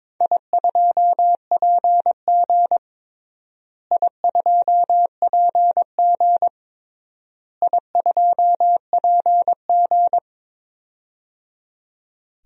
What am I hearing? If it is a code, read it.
Morse code, heard as a keyed tone, about 22 wpm, 700 Hz. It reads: I2PG I2PG I2PG
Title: DXCC-100 Call Signs - 4 Characters - Repeated 3x - 22wpm